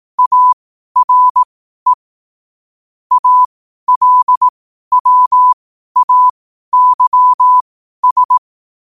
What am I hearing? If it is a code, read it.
Morse code, heard as a keyed tone, about 18 wpm, spaced as for 12 wpm, 1000 Hz. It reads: ARE ALWAYS
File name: qotdfiles / 12wpm